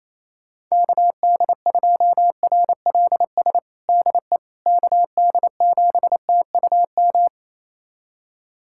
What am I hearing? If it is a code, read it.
Morse code, heard as a keyed tone, about 28 wpm, 700 Hz. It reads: KD2RLS DE KD7TUM
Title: USA Callsign DE Callsign - 6 Characters - 28wpm